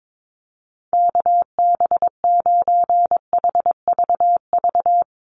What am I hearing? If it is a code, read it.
Morse code, heard as a keyed tone, about 22 wpm, 700 Hz. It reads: KB9HVV
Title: USA Call Signs Repeated 3x 22wpm